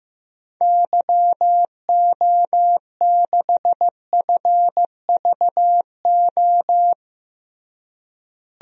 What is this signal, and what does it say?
Morse code, heard as a keyed tone, about 15 wpm, 700 Hz. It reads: YO6FVO
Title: DXCC-100 Call Signs - 6 Characters - Repeated 3x - 15wpm